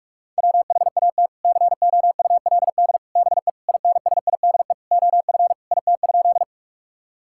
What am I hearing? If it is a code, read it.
Morse code, heard as a keyed tone, about 46 wpm, 700 Hz. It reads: WHAT COULD BE INSIDE OF IT?
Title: Sentences from Top 1000 Words - Review Part 6 - 45wpm